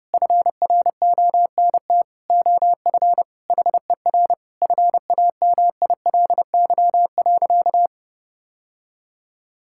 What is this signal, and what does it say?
Morse code, heard as a keyed tone, about 32 wpm, 700 Hz. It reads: FRONT OF HER FAMILY.
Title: Sentences from Top 1000 Words - Review Part 5 - 30wpm